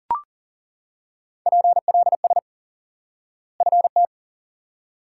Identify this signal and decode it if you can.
Morse code, heard as a keyed tone, about 39 wpm, 700 Hz. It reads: PLS FT
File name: IQE Course 3 - Lesson 89 - YAGI - Two Elements - 40wpm